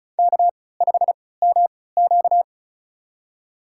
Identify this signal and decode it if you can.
Morse code, heard as a keyed tone, about 36 wpm, spaced as for 20 wpm, 700 Hz. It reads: K5MQ
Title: USA Call Signs - 4 Characters - 3x Character Spacing - 35wpm